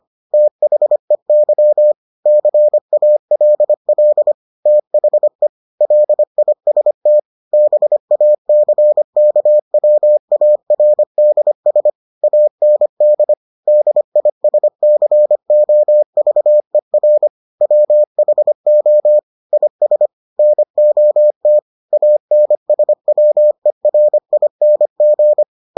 Morse code, heard as a keyed tone, about 25 wpm, 600 Hz. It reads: THEY CALL THE LIST BACKWARDS AND DISCOVER WHO IS NOT ANSWERING